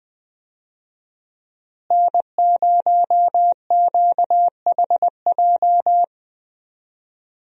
Morse code, heard as a keyed tone, about 20 wpm, 700 Hz. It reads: N0QHJ
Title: USA Call Signs - 5 Characters - Repeated 2x - 20wpm